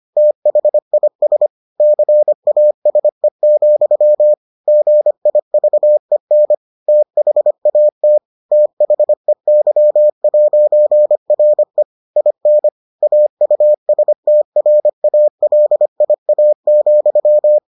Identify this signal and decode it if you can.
Morse code, heard as a keyed tone, about 25 wpm, 600 Hz. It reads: THIS CASE, GIVEN THAT THEY'RE IN AUSTRALIA,